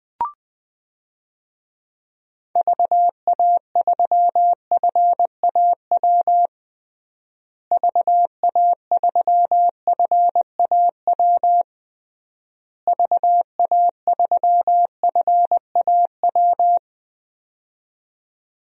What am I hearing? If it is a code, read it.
Morse code, heard as a keyed tone, about 20 wpm, 700 Hz. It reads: VA3FAW VA3FAW VA3FAW
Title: DXCC-100 Call Signs - 6 Characters - Repeated 3x - 20wpm